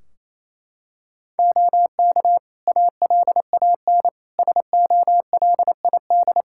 Morse code, heard as a keyed tone, about 28 wpm, 700 Hz. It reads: OK ALAN SOLID
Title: Practice QSO - Exchange 3 - More Info - 28wpm